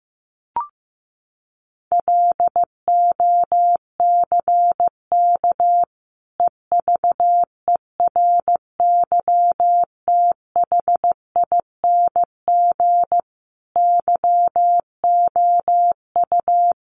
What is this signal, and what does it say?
Morse code, heard as a keyed tone, about 15 wpm, 700 Hz. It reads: LOCK EVERYTHING YOU
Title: Sentences from Top 1300 Words 15wpm